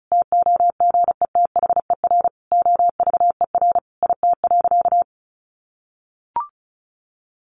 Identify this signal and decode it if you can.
Morse code, heard as a keyed tone, about 34 wpm, 700 Hz. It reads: TOGETHER OVER IT.
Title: Sentences from Top 1500 Words 35wpm